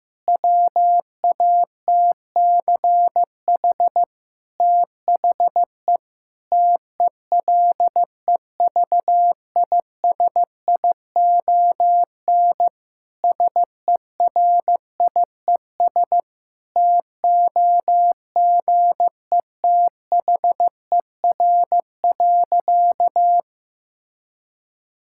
Morse code, heard as a keyed tone, about 15 wpm, 700 Hz. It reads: WATCH THE TELEVISION SERIES TOGETHER.
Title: Sentences from Top 1000 Words - Review Part 5 - 15wpm